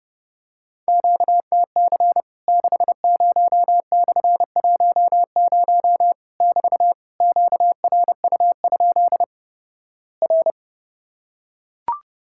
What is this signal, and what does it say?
Morse code, heard as a keyed tone, about 30 wpm, 700 Hz. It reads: QTC 60/10 <BT> QRU? R
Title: Worked All Europe DX Contest - QTC Exchange - 30wpm